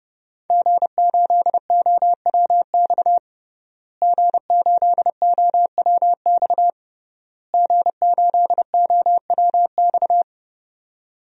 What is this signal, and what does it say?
Morse code, heard as a keyed tone, about 30 wpm, 700 Hz. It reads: G8OWX G8OWX G8OWX
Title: DXCC-100 Call Signs - 5 Characters - Repeated 3x - 30wpm